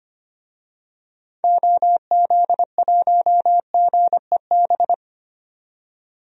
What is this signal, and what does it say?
Morse code, heard as a keyed tone, about 25 wpm, 700 Hz. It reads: OZ1GEB
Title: DXCC-100 Call Signs - 6 Characters - 25wpm